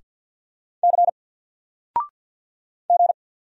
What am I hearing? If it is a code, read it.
Morse code, heard as a keyed tone, about 48 wpm, 700 Hz. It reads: C G
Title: Single Letter-Number 50wpm